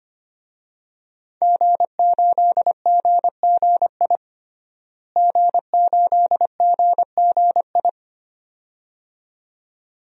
Morse code, heard as a keyed tone, about 25 wpm, 700 Hz. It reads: G8GGI G8GGI
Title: DXCC-100 Call Signs - 5 Characters - Repeated 2x - 25wpm